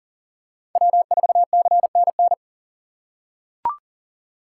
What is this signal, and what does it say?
Morse code, heard as a keyed tone, about 40 wpm, 700 Hz. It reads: W4CNN
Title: USA Call Signs - 4x Character Spacing - 40wpm